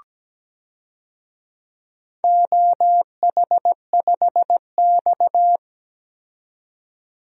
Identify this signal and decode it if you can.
Morse code, heard as a keyed tone, about 17 wpm, 700 Hz. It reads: OH5X
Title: DXCC-100 Call Signs - 4 Characters - 17wpm